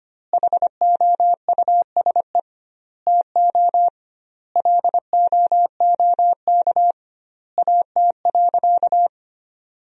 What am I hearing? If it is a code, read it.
Morse code, heard as a keyed tone, about 25 wpm, 700 Hz. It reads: HOUSE TO LOOK AT.
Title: 2x Word-Spacing - Sentences from Top 200 Words - 25wpm